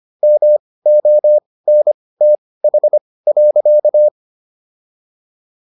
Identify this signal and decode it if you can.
Morse code, heard as a keyed tone, about 25 wpm, spaced as for 18 wpm, 600 Hz. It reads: MONTH.